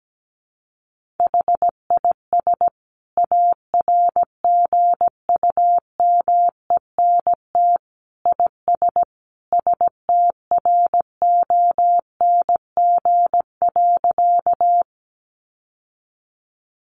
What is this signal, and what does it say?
Morse code, heard as a keyed tone, about 17 wpm, 700 Hz. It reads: HIS ARGUMENT IS STRONG.
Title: Sentences from Top 1400 Words 17wpm